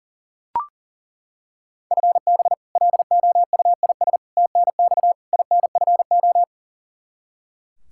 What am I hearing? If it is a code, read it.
Morse code, heard as a keyed tone, about 40 wpm, 700 Hz. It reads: FB LOUIS TNX INFO
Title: K1USN SST QSO Practice 40wpm